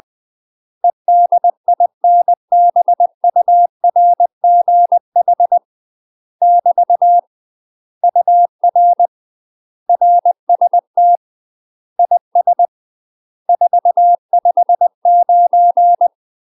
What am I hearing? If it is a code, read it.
Morse code, heard as a keyed tone, about 20 wpm, 700 Hz. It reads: EDINBURGH <BT> UR RST IS 459
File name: UK MASTER QSOs TRAINING / 20WPM / UK MASTER QSOs 0069